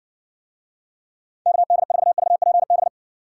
Morse code, heard as a keyed tone, about 60 wpm, 700 Hz. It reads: KD4VPB